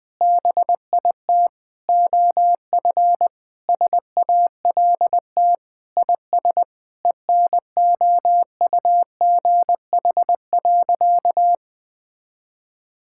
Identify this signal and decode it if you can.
Morse code, heard as a keyed tone, about 21 wpm, 700 Hz. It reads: BIT OF SALT IS ENOUGH.
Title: Sentences with 4-Word Phrases - 20wpm